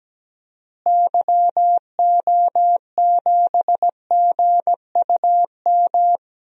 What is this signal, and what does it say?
Morse code, heard as a keyed tone, about 17 wpm, 700 Hz. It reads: YO7GUM
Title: DXCC-100 Call Signs - 6 Characters - Repeated 2x - 17wpm